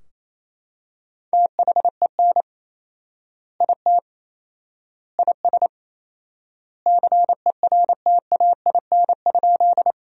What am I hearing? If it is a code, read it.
Morse code, heard as a keyed tone, about 28 wpm, 700 Hz. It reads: THEN IT IS CERTAIN?
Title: 4x Word-Spacing - Sentences from Top 500 Words - 28wpm